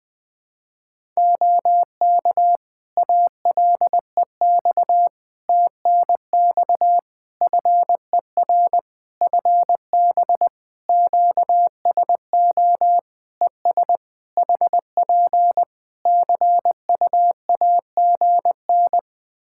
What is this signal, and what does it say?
Morse code, heard as a keyed tone, about 20 wpm, 700 Hz. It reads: OK ALEX TNX FER FB QSO ES HP CUAGN
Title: Practice QSO - Exchange 4 - Ending - 20wpm